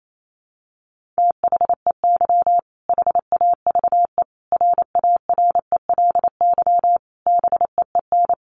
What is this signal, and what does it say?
Morse code, heard as a keyed tone, about 28 wpm, 700 Hz. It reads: THEY HAVE RARELY BEEN
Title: Sentences from Top 1400 Words 28wpm